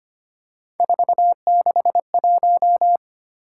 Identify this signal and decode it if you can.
Morse code, heard as a keyed tone, about 25 wpm, 700 Hz. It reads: 461
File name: Three Letter-Number Combinations - 25wpm